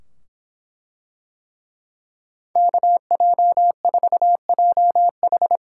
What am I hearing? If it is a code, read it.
Morse code, heard as a keyed tone, about 24 wpm, 700 Hz. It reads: KJ4JH